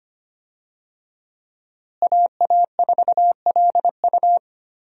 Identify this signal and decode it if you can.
Morse code, heard as a keyed tone, about 25 wpm, 700 Hz. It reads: AA4LU
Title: USA Call Signs - 5 Characters - Repeated 3x - 25wpm